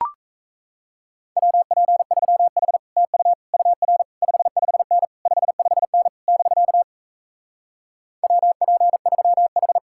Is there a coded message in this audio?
Yes: Morse code from 1.4 to 9.8 s, heard as a keyed tone, about 38 wpm, 700 Hz. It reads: WP3H TU UR 55N 55N <BK> WP3H